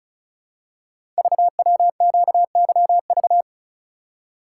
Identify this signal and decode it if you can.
Morse code, heard as a keyed tone, about 36 wpm, 700 Hz. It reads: VWQYV
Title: Code Groups - 5 Letters - 35wpm